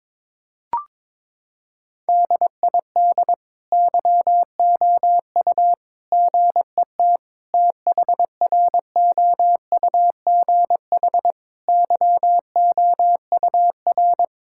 Morse code, heard as a keyed tone, about 22 wpm, 700 Hz. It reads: DID YOU GET THROUGH YOUR
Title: Sentences from Top 1000 Words - Review Part 5 - 22wpm